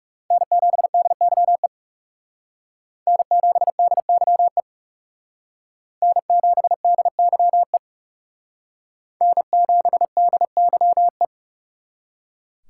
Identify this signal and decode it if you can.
Morse code, heard as a keyed tone, about 38 wpm, 700 Hz. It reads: N7DYE N7DYE N7DYE N7DYE
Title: Speed Racing - USA 1x3 Call Signs 30wpm